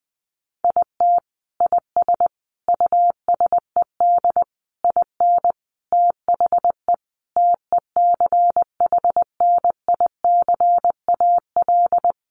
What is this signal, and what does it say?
Morse code, heard as a keyed tone, about 19 wpm, 700 Hz. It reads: IT IS USED IN THE TECHNICAL
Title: Sentences from Top 1100 Words 20wpm